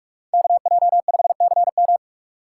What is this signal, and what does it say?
Morse code, heard as a keyed tone, about 58 wpm, 700 Hz. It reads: KJ5CM